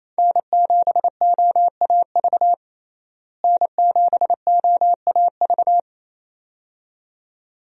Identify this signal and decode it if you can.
Morse code, heard as a keyed tone, about 28 wpm, 700 Hz. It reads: N7OAV N7OAV
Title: USA Call Signs - 5 Characters - Repeated 2x - 28wpm